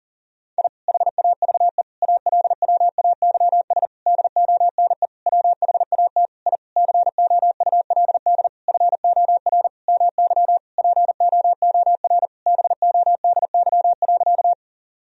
Morse code, heard as a keyed tone, about 40 wpm, 700 Hz. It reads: I HAVE ALWAYS DONE WHAT I COULD FOR MY POOR BODY.
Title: Sentences from Top 1000 Words - Review Part 2 - 40wpm